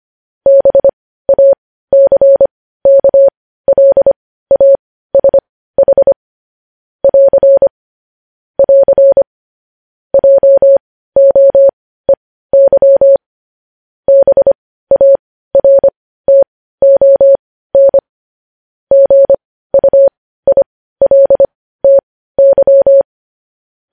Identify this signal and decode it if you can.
Morse code, heard as a keyed tone, about 25 wpm, spaced as for 15 wpm, 550 Hz. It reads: BACKLASH <AR> <AR> JOEY BARTON GUILTY
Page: news-headlines-in-morse-code-at-15-wpm